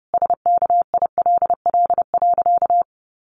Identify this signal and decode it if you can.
Morse code, heard as a keyed tone, about 30 wpm, 700 Hz. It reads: SKILL.
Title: Sentences from Top 1200 Words 30wpm